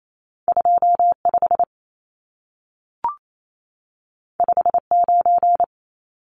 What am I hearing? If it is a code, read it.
Morse code, heard as a keyed tone, about 28 wpm, 700 Hz. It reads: 25 59
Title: Numbers - 2-Digit 28wpm